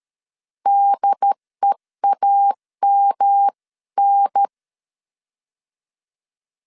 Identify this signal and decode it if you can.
Morse code, heard as a keyed tone, about 13 wpm, 800 Hz. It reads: DEAMN
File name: CW / DET / 1 3 1500C S A 12wpm 2905